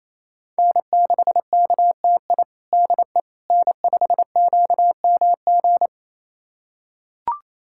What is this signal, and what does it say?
Morse code, heard as a keyed tone, about 28 wpm, 700 Hz. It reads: N6KTI DE N5QMG
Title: USA Callsign DE Callsign - 5 Characters - 28wpm